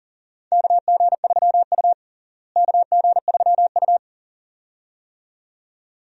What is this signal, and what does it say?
Morse code, heard as a keyed tone, about 37 wpm, 700 Hz. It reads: KG3U KG3U
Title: USA Call Signs - 4 Characters - Repeated 2x - 40wpm